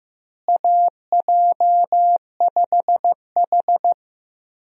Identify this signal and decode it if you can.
Morse code, heard as a keyed tone, about 15 wpm, 700 Hz. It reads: AJ5H
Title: USA Call Signs - 4 Characters - Repeated 2x - 15wpm